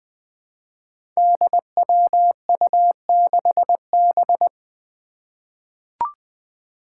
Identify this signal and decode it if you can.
Morse code, heard as a keyed tone, about 20 wpm, 700 Hz. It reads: DWU6B
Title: Five Challenging Letter-Number Combinations 20wpm